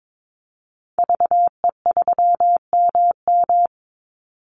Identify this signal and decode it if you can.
Morse code, heard as a keyed tone, about 22 wpm, 700 Hz. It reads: VE3MM
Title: Call Signs - 22wpm